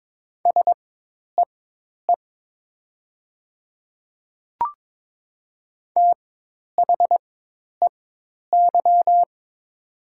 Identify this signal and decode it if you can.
Morse code, heard as a keyed tone, about 22 wpm, spaced as for 10 wpm, 700 Hz. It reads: SEE THEY
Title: 4x Character Spacing - Top 200 Words - 22wpm